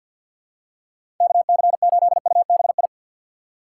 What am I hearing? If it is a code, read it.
Morse code, heard as a keyed tone, about 53 wpm, 700 Hz. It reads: KC8UBI
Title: USA Call Signs - 6 Characters - 4x Character Spacing - 50wpm